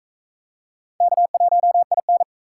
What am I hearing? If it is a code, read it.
Morse code, heard as a keyed tone, about 38 wpm, 700 Hz. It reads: K1IN